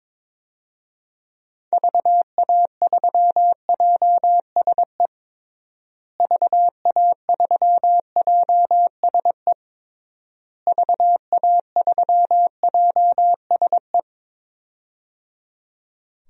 Morse code, heard as a keyed tone, about 22 wpm, 700 Hz. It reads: VA3JSE VA3JSE VA3JSE
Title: DXCC-100 Call Signs - 6 Characters - Repeated 3x - 22wpm